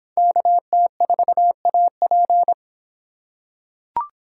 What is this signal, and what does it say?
Morse code, heard as a keyed tone, about 25 wpm, 700 Hz. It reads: KT4AP